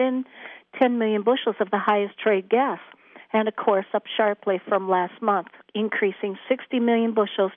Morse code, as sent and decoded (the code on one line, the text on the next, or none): none